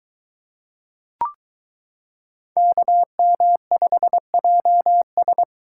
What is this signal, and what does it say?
Morse code, heard as a keyed tone, about 22 wpm, 700 Hz. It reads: KM5JS